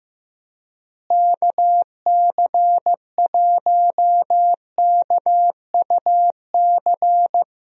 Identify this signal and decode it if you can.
Morse code, heard as a keyed tone, about 15 wpm, 700 Hz. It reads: KC1KUC
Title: USA Call Signs - 6 Characters - Repeated 3x - 15wpm